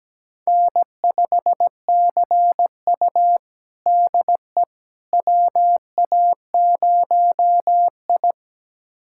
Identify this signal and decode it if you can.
Morse code, heard as a keyed tone, about 17 wpm, 700 Hz. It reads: N5CU DE WA0I
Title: USA Callsign DE Callsign - 4 Characters - 17wpm